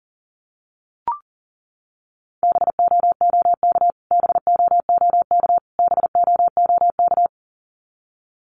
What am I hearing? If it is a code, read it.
Morse code, heard as a keyed tone, about 40 wpm, 700 Hz. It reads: BOOK BOOK BOOK
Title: Top 500 Words Repeated 40wpm